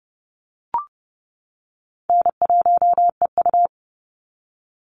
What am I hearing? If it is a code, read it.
Morse code, heard as a keyed tone, about 30 wpm, 700 Hz. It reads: N1EU
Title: NA SOTA Activator Callsigns 30wpm